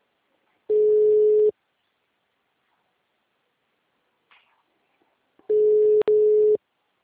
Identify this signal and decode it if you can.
Ringback tone (Europe, 425 Hz).
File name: out- -2046- - - .35388